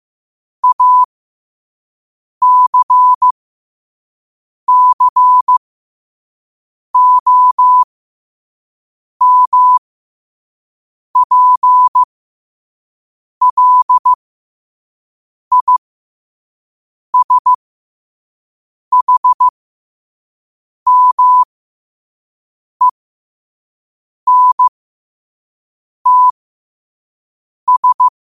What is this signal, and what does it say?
Morse code, heard as a keyed tone, about 15 wpm, spaced as for 5 wpm, 1000 Hz. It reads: ACCOMPLISHMENTS
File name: qotdfiles / 5wpm